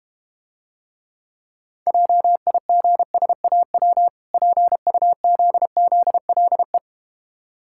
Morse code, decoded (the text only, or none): JIGSAW PUZZLE